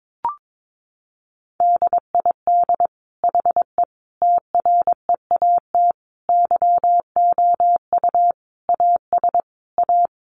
Morse code, heard as a keyed tone, about 22 wpm, 700 Hz. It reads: DID HE TREAT YOU AS A
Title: Sentences from Top 1300 Words 22wpm